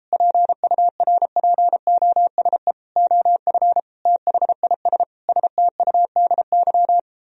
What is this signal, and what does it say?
Morse code, heard as a keyed tone, about 33 wpm, 700 Hz. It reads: PURPOSE OF THIS STUDY